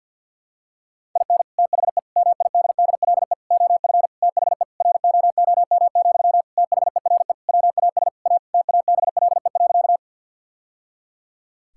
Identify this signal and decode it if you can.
Morse code, heard as a keyed tone, about 50 wpm, 700 Hz. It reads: IN THE MIDDLE OF THE ROOM, THERE WAS A TABLE.
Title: Sentences from Top 1000 Words - Review Part 6 - 50wpm